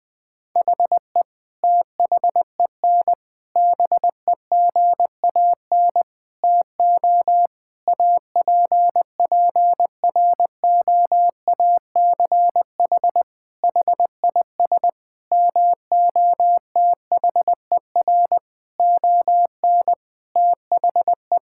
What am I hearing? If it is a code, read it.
Morse code, heard as a keyed tone, about 20 wpm, 700 Hz. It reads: HE THEN BEGAN TO APPROACH HIS MOTHER ON THE
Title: Sentences from Top 1000 Words - Review Part 6 - 20wpm